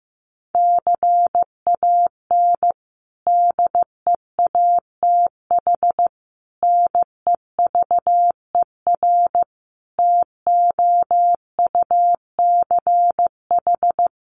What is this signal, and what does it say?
Morse code, heard as a keyed tone, about 15 wpm, 700 Hz. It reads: CAN DEATH NEVER TOUCH
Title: Sentences from Top 1200 Words 15wpm